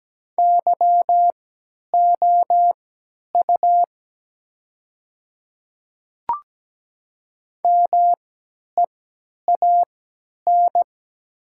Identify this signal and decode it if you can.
Morse code, heard as a keyed tone, about 17 wpm, spaced as for 10 wpm, 700 Hz. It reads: YOU MEAN
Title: 3x Character Spacing - Top 200 Words - 17wpm